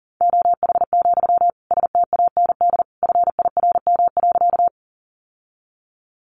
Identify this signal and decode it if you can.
Morse code, heard as a keyed tone, about 40 wpm, 700 Hz. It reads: OH, STAND FIRM.
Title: Sentences from Top 1400 Words 40wpm